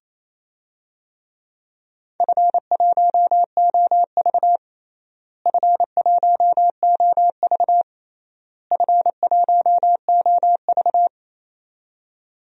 Morse code, heard as a keyed tone, about 28 wpm, 700 Hz. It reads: F1OV F1OV F1OV
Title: DXCC-100 Call Signs - 4 Characters - Repeated 3x - 28wpm